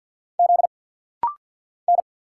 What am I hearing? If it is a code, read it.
Morse code, heard as a keyed tone, about 50 wpm, 700 Hz. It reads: Z N
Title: Single Letters - Rapid-Fire 50wpm